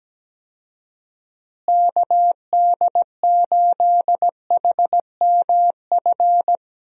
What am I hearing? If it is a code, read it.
Morse code, heard as a keyed tone, about 17 wpm, 700 Hz. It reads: KD8HMF
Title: USA Call Signs - 6 Characters - Repeated 2x - 17wpm